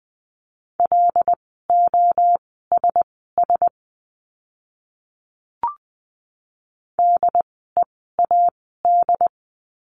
Morse code, heard as a keyed tone, about 20 wpm, spaced as for 15 wpm, 700 Hz. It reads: LOSS DEAD
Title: 075 Lesson 13f - Letter L - Words 20wpm